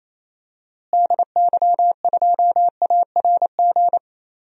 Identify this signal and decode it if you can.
Morse code, heard as a keyed tone, about 27 wpm, 700 Hz. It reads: DY2ARG
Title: DXCC-100 Call Signs - 6 Characters - Repeated 3x - 28wpm